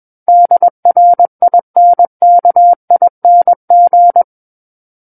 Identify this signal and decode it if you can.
Morse code, heard as a keyed tone, about 21 wpm, 700 Hz. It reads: DRINKING
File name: Book One / 12-Chapter 12 - 21 WPM